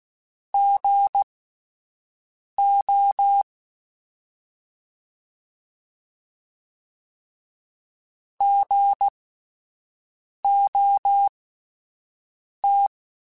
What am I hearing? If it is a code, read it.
Morse code, heard as a keyed tone, about 16 wpm, spaced as for 5 wpm, 800 Hz. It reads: GO GOT
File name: Spaced Alphas / Spaced Alphas mp3s / Gs sp @10